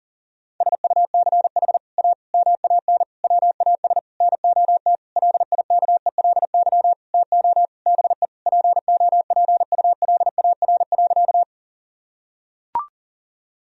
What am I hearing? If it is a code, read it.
Morse code, heard as a keyed tone, about 40 wpm, 700 Hz. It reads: SUCH A MAN WAS NOT LIKELY TO BE POPULAR.
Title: Sentences from Top 1000 Words - Review Part 4 - 40wpm